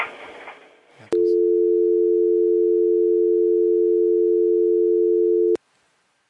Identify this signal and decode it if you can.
Dial tone (North America), 350 + 440 Hz.